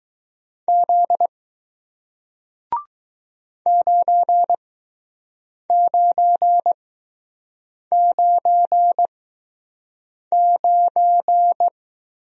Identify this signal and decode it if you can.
Morse code, heard as a keyed tone, about 20 wpm, spaced as for 6 wpm, 700 Hz. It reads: Z 9999